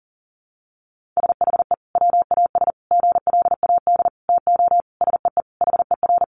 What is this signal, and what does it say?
Morse code, heard as a keyed tone, about 40 wpm, 700 Hz. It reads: SHE WAS GLAD TO SEE HER
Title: Sentences from Top 800 Words 40wpm